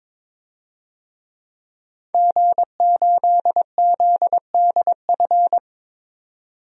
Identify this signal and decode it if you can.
Morse code, heard as a keyed tone, about 22 wpm, 700 Hz. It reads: G8ZDF
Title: DXCC-100 Call Signs - 5 Characters - 22wpm